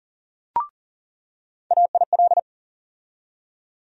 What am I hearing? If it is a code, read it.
Morse code, heard as a keyed tone, about 40 wpm, 700 Hz. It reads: AIL
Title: USA Call Sign Postfixes - 3 Letters 40wpm